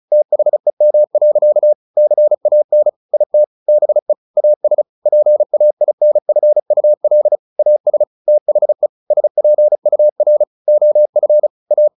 Morse code, heard as a keyed tone, about 34 wpm, 600 Hz. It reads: THEM. CAN IT BE AS PAINFUL AS THE SPUR OF A
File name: the old man and the sea / the old man and the sea 1 35wpm